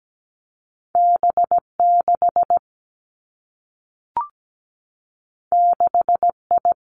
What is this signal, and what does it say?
Morse code, heard as a keyed tone, about 17 wpm, 700 Hz. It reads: B6 6I
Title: Two Challenging Letter-Number Combinations 17wpm